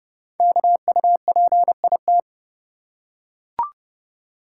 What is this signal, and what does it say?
Morse code, heard as a keyed tone, about 29 wpm, 700 Hz. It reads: KUPIT
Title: Code Groups - 5 Letter:Numbers - 30wpm